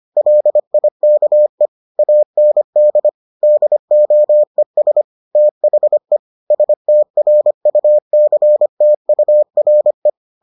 Morse code, heard as a keyed tone, about 25 wpm, 600 Hz. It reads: LIKE AND DOES THE STRUCTURE